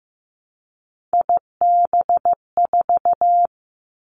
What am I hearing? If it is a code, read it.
Morse code, heard as a keyed tone, about 15 wpm, 700 Hz. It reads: IB4
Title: Three Challenging Letter-Number Combinations 15wpm